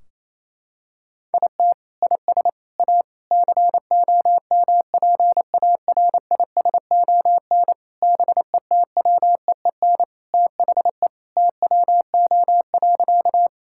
Morse code, heard as a keyed tone, about 28 wpm, 700 Hz. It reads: IT IS A COMPARISON BETWEEN THE TWO.
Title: Sentences from Top 1600 Words 28wpm